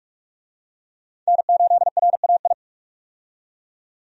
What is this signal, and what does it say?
Morse code, heard as a keyed tone, about 44 wpm, 700 Hz. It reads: N8RAI
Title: USA Call Signs - 5 Characters - 45wpm